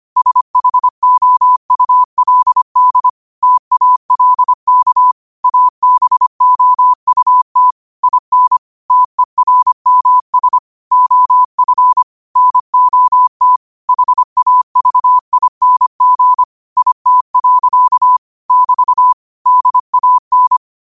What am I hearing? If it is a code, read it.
Morse code, heard as a keyed tone, about 25 wpm, 1000 Hz. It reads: SHOULD TALK ABOUT IN TERMS OF NOT HAVING IT. <BT> DAN
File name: qotdfiles / 25wpm